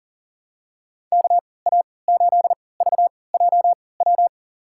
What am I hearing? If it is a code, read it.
Morse code, heard as a keyed tone, about 40 wpm, spaced as for 23 wpm, 700 Hz. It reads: KA8VJW
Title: USA Call Signs - 3x Character Spacing - 40wpm